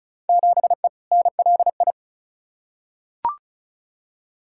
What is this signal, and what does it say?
Morse code, heard as a keyed tone, about 36 wpm, 700 Hz. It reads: 7E NLI
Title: Field Day Exchanges Encore - 35wpm